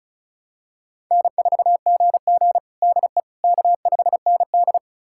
Morse code, heard as a keyed tone, about 36 wpm, 700 Hz. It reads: N4GG DE K5ND
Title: USA Callsign DE Callsign - 4 Characters - 35wpm